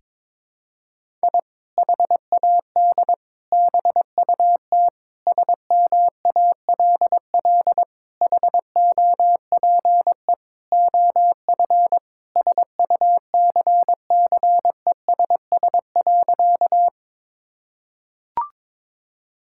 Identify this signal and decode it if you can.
Morse code, heard as a keyed tone, about 22 wpm, 700 Hz. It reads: I HAD BUT SMALL HOPE OF SUCCESS.
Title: Sentences from Top 1000 Words - Review Part 6 - 22wpm